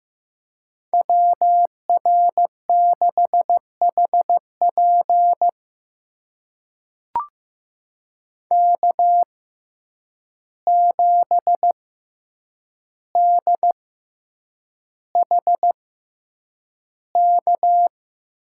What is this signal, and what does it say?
Morse code, heard as a keyed tone, about 15 wpm, spaced as for 8 wpm, 700 Hz. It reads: WR6HP K7DHK
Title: USA Call Signs - 5 Characters - 6x Character Spacing - 15wpm